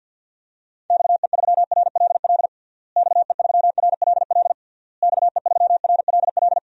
Morse code, heard as a keyed tone, about 52 wpm, 700 Hz. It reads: XE3RLL XE3RLL XE3RLL
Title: DXCC-100 Call Signs - 6 Characters - Repeated 3x - 50wpm